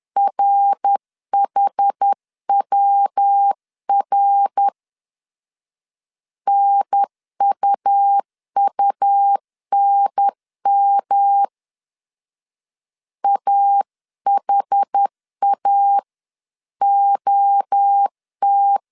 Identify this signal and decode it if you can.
Morse code, heard as a keyed tone, about 10 wpm, 800 Hz. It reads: RHWR NUUNM AHA OT
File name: 10wpm